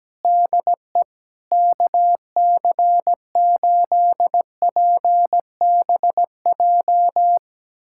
Morse code, heard as a keyed tone, about 17 wpm, 700 Hz. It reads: DE KC8PBJ
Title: USA Callsign DE Callsign 17wpm